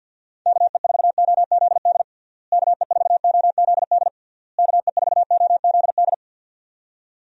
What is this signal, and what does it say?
Morse code, heard as a keyed tone, about 50 wpm, 700 Hz. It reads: KE4OZD KE4OZD KE4OZD
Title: USA Call Signs - 6 Characters - Repeated 3x - 50wpm